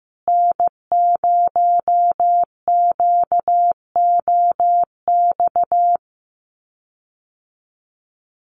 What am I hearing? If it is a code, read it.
Morse code, heard as a keyed tone, about 15 wpm, 700 Hz. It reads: N0QOX
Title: USA Call Signs Repeated 3x 15wpm